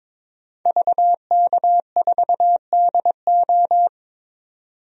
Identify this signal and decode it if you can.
Morse code, heard as a keyed tone, about 22 wpm, 700 Hz. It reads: VK4DO
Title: DXCC-100 Call Signs - 5 Characters - 22wpm